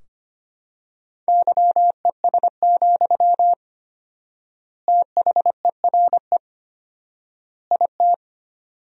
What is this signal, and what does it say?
Morse code, heard as a keyed tone, about 25 wpm, 700 Hz. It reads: YES, THERE IT
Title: 4x Word-Spacing - Sentences from Top 500 Words - 25wpm